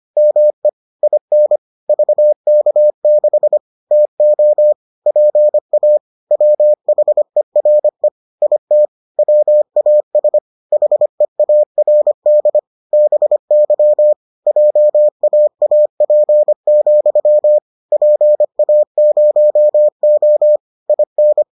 Morse code, heard as a keyed tone, about 25 wpm, 600 Hz. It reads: ME IN VK6 TO PA WHERE IT WAS HEARD BY JAAP, PA0O IN